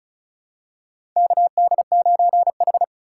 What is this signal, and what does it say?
Morse code, heard as a keyed tone, about 36 wpm, 700 Hz. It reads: KD9H
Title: USA Call Signs - 4x Character Spacing - 35wpm